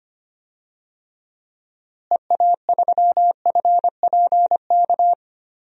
Morse code, heard as a keyed tone, about 25 wpm, 700 Hz. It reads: EA3FPK
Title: DXCC-100 Call Signs - 6 Characters - Repeated 2x - 25wpm